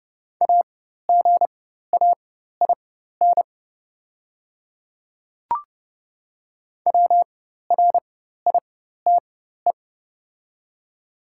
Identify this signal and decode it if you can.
Morse code, heard as a keyed tone, about 30 wpm, spaced as for 14 wpm, 700 Hz. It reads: AGAIN WRITE
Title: 4x Character Spacing - Top 500 Words - 30wpm